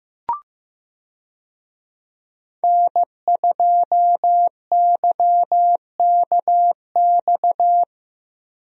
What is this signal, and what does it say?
Morse code, heard as a keyed tone, about 15 wpm, 700 Hz. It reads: N2YKX
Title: USA Call Signs - 5 Characters - Repeated 2x - 15wpm